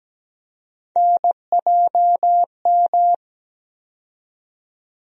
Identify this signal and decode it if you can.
Morse code, heard as a keyed tone, about 17 wpm, 700 Hz. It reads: NJM
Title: USA Call Sign Postfixes - 3 Letters 17wpm